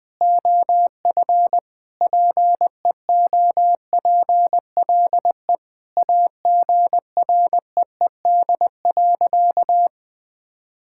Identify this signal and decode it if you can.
Morse code, heard as a keyed tone, about 20 wpm, 700 Hz. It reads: OF PEOPLE AGREED.
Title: Sentences with 3-Word Phrases - 20wpm